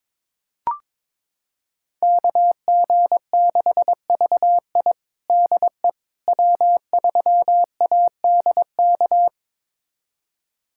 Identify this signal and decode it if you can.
Morse code, heard as a keyed tone, about 22 wpm, 700 Hz. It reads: KG6VI DE W3ADK
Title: USA Callsign DE Callsign - 5 Characters - 22wpm